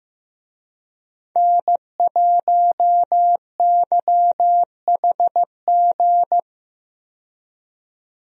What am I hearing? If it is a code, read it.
Morse code, heard as a keyed tone, about 15 wpm, 700 Hz. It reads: N1YHG
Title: USA Call Signs - 5 Characters - 15wpm